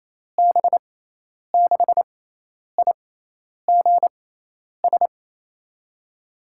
Morse code, heard as a keyed tone, about 28 wpm, spaced as for 10 wpm, 700 Hz. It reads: B6IGS
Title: USA Call Signs - 6x Character Spacing - 28wpm